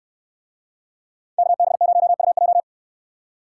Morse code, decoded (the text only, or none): KB0UJ